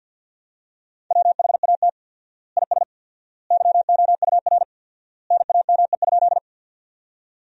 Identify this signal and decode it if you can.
Morse code, heard as a keyed tone, about 50 wpm, 700 Hz. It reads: WHAT IS YOUR NAME?
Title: 4x Word-Spacing - Sentences from Top 200 Words - 50wpm